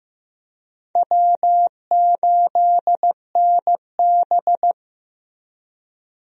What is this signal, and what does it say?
Morse code, heard as a keyed tone, about 16 wpm, 700 Hz. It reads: W8NB
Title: USA Call Signs - 4 Characters - Repeated 2x - 15wpm